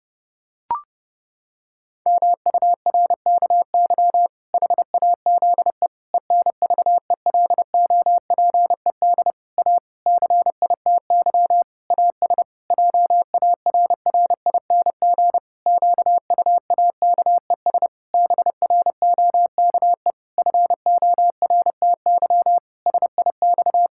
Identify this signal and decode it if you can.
Morse code, heard as a keyed tone, about 31 wpm, 700 Hz. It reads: MURKY HAZE ENVELOPED A CITY AS JARRING QUAKES BROKE FORTY SIX
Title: Pangram Challenge v2 - 30wpm